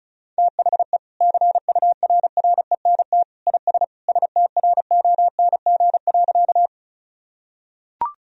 Morse code, heard as a keyed tone, about 36 wpm, 700 Hz. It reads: THE CURRENT IS STRONG.
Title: Sentences from Top 1000 Words - Review Part 4 - 35wpm